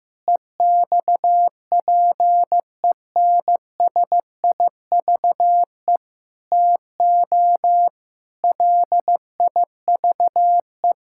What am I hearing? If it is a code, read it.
Morse code, heard as a keyed tone, about 15 wpm, 700 Hz. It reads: EXPENSIVE TO LIVE
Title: Sentences from Top 1000 Words - Review Part 6 - 15wpm